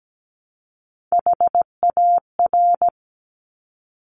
Morse code, heard as a keyed tone, about 17 wpm, 700 Hz. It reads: HAR
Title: Top Three Letter Combinations in Words 17wpm